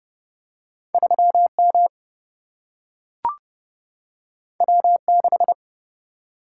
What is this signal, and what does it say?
Morse code, heard as a keyed tone, about 31 wpm, 700 Hz. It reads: 3 M W 6
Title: Single-Letter Single-Number 30wpm